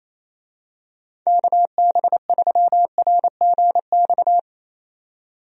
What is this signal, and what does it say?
Morse code, heard as a keyed tone, about 28 wpm, 700 Hz. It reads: KB3RGX
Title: USA Call Signs - 6 Characters - 28wpm